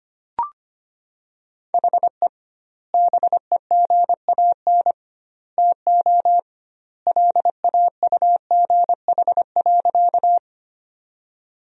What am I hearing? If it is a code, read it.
Morse code, heard as a keyed tone, about 25 wpm, 700 Hz. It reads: HE BEGAN TO LAUGH.
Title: 2x Word-Spacing - Sentences from Top 500 Words - 25wpm